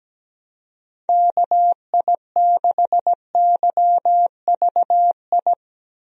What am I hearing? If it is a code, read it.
Morse code, heard as a keyed tone, about 17 wpm, 700 Hz. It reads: KI6YVI
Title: USA Call Signs - 6 Characters - Repeated 2x - 17wpm